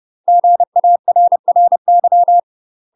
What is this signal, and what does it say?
Morse code, heard as a keyed tone, about 30 wpm, 700 Hz. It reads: GARRY
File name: UK MASTER QSOs TRAINING / 30WPM / UK MASTER QSOs 0028